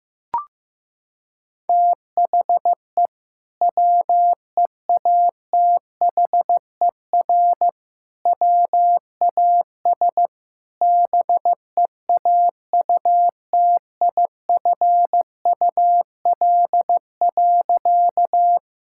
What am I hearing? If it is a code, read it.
Morse code, heard as a keyed tone, about 15 wpm, 700 Hz. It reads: THE WEATHER WAS BEAUTIFUL.
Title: Sentences from Top 1000 Words - Review Part 5 - 15wpm